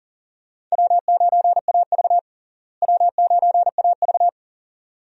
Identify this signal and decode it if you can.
Morse code, heard as a keyed tone, about 40 wpm, 700 Hz. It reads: W9AV W9AV
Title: USA Call Signs - 4 Characters - Repeated 2x - 40wpm